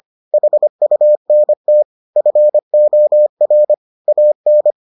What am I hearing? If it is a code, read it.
Morse code, heard as a keyed tone, about 25 wpm, 600 Hz. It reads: HUNT FOR AN